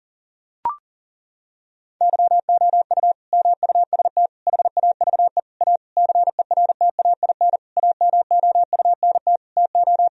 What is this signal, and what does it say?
Morse code, heard as a keyed tone, about 37 wpm, 700 Hz. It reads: YOU MUST HAVE A CERTAIN AMOUNT TO
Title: Sentences from Top 1000 Words - Review Part 4 - 40wpm